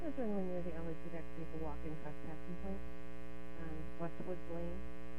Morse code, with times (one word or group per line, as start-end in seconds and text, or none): none